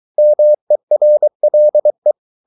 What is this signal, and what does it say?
Morse code, heard as a keyed tone, about 23 wpm, 600 Hz. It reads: MERLE
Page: Ferie 23 wpm